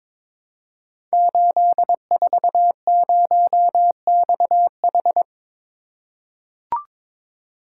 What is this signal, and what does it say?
Morse code, heard as a keyed tone, about 22 wpm, 690 Hz. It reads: 840XH
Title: Code Groups - 5 Letter:Numbers - 22wpm